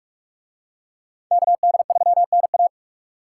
Morse code, heard as a keyed tone, about 44 wpm, 700 Hz. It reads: KD3NA